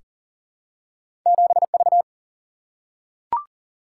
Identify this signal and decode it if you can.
Morse code, heard as a keyed tone, about 40 wpm, 700 Hz. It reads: 7V
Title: Single-Letter Single-Number 40wpm